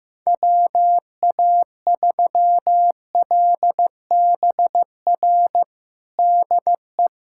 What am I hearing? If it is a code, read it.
Morse code, heard as a keyed tone, about 15 wpm, 700 Hz. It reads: WA3LBR DE
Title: USA Callsign DE Callsign - 6 Characters - 15wpm